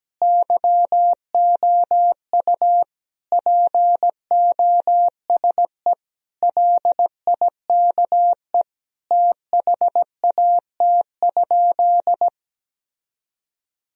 Morse code, heard as a keyed tone, about 17 wpm, 700 Hz. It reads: YOU POSE LIKE THAT?
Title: Sentences from Top 1000 Words - Review Part 2 - 17wpm